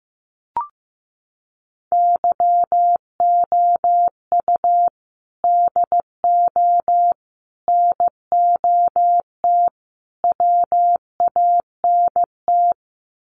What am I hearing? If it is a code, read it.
Morse code, heard as a keyed tone, about 15 wpm, 700 Hz. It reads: YOU DO NOT WANT
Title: Sets of 4 Words 15wpm